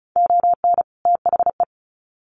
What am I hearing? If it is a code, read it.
Morse code, heard as a keyed tone, about 34 wpm, 700 Hz. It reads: ON THE
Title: Sets of 2 Words 35wpm